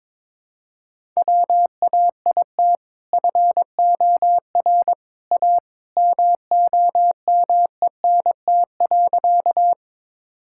Morse code, heard as a keyed tone, about 22 wpm, 700 Hz. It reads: WAIT FOR A MOMENT.
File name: practice / 22wpm / Sentences with 2-Word Phrases - 22wpm